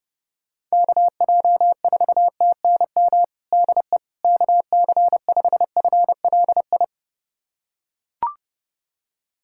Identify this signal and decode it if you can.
Morse code, heard as a keyed tone, about 32 wpm, 700 Hz. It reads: KJ4TNM DE KC5FLI
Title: USA Callsign DE Callsign - 6 Characters - 30wpm